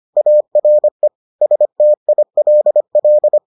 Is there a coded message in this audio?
Yes: Morse code from 0.2 to 3.4 s, heard as a keyed tone, about 25 wpm, 600 Hz. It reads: ARE STILL